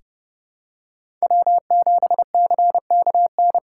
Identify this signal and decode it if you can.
Morse code, heard as a keyed tone, about 32 wpm, 700 Hz. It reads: W7CKN
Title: USA Call Signs - 5 Characters - 4x Character Spacing - 30wpm